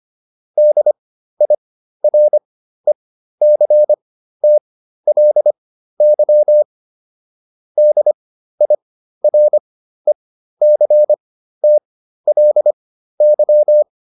Morse code, heard as a keyed tone, about 25 wpm, spaced as for 13 wpm, 600 Hz. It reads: DIRECTLY DIRECTLY